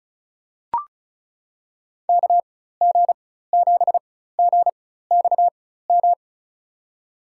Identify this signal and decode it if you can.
Morse code, heard as a keyed tone, about 34 wpm, spaced as for 16 wpm, 700 Hz. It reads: KG7GXM
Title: USA Call Signs - 6 Characters - 4x Character Spacing - 35wpm